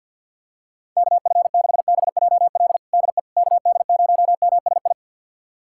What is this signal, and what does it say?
Morse code, heard as a keyed tone, about 47 wpm, 700 Hz. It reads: KF6BJL DE KD0MSI
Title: USA Callsign DE Callsign - 6 Characters - 50wpm